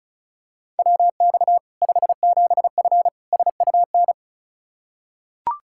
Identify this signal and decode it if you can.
Morse code, heard as a keyed tone, about 36 wpm, 700 Hz. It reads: WX 57F SUN
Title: K1USN SST QSO Practice 35wpm